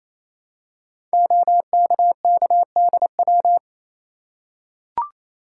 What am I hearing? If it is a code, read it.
Morse code, heard as a keyed tone, about 27 wpm, 700 Hz. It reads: OKKDW
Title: Code Groups - 5 Letter:Numbers - 28wpm